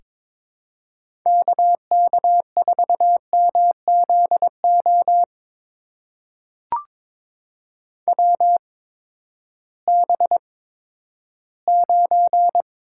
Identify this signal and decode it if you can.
Morse code, heard as a keyed tone, about 22 wpm, spaced as for 12 wpm, 700 Hz. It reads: KK4MZO WB9
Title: USA Call Signs - 6 Characters - 8x Character Spacing - 22wpm